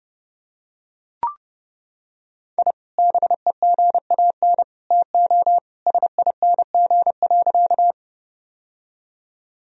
Morse code, heard as a keyed tone, about 30 wpm, 700 Hz. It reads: I BEGAN TO SING.
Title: Sentences from Top 1000 Words - Review Part 2 - 30wpm